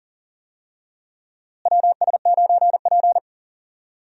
Morse code, heard as a keyed tone, about 38 wpm, 700 Hz. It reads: WS9P
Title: USA Call Signs - 4 Characters - Repeated 2x - 40wpm